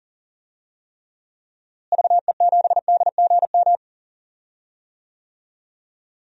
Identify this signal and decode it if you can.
Morse code, heard as a keyed tone, about 40 wpm, 700 Hz. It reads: VE7DGM
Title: DXCC-100 Call Signs - 6 Characters - Repeated 2x - 40wpm